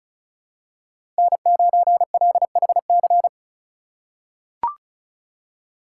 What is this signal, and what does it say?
Morse code, heard as a keyed tone, about 34 wpm, 700 Hz. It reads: N9LHC
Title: USA Call Signs - 5 Characters - 3x Character Spacing - 35wpm